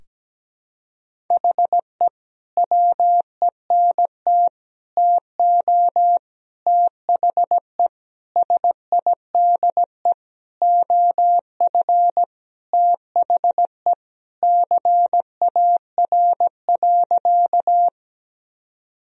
Morse code, heard as a keyed tone, about 17 wpm, 700 Hz. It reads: HE WENT TO THE SIDE OF THE CAR.
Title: Sentences from Top 1000 Words - Review Part 1 - 17wpm